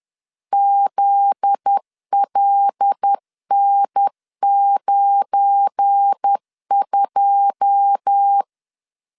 Morse code, heard as a keyed tone, about 10 wpm, 800 Hz. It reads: ZLN92
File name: LEZIONI / 9 28 10wpm 2842